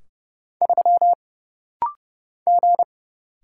Morse code, heard as a keyed tone, about 30 wpm, 700 Hz. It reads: 3 G
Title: Single Letter-Number - Rapid-Fire 30wpm